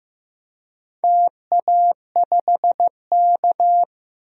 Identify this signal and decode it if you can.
Morse code, heard as a keyed tone, about 15 wpm, 700 Hz. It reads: TA5K
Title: DXCC-100 Call Signs - 4 Characters - 15wpm